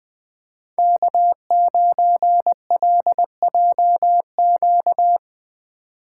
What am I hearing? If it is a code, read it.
Morse code, heard as a keyed tone, about 20 wpm, 700 Hz. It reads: K9LJQ
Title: USA Call Signs - 5 Characters - 3x Character Spacing - 20wpm